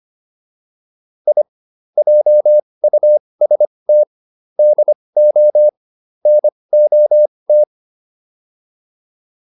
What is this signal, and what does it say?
Morse code, heard as a keyed tone, about 25 wpm, spaced as for 20 wpm, 600 Hz. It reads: I JUST DO NOT